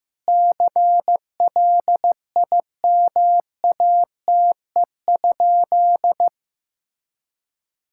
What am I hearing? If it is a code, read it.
Morse code, heard as a keyed tone, about 16 wpm, 700 Hz. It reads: CLIMATE?
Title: Sentences from Top 1600 Words 15wpm